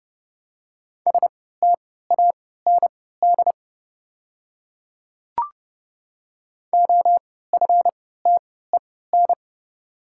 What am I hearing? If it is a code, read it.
Morse code, heard as a keyed tone, about 30 wpm, spaced as for 17 wpm, 700 Hz. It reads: STAND OFTEN
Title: 3x Character Spacing - Top 500 Words - 30wpm